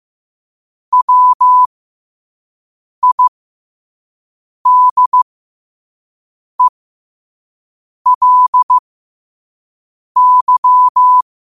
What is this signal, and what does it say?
Morse code, heard as a keyed tone, about 15 wpm, spaced as for 5 wpm, 1000 Hz. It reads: WIDELY
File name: qotdfiles / 5wpm